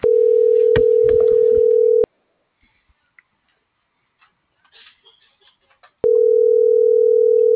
Ringback tone (North America), 440 + 480 Hz.